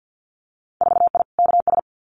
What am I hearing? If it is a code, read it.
Morse code, heard as a keyed tone, about 50 wpm, 700 Hz. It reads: 4I KS
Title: Winter Field Day Exchanges Encore 50wpm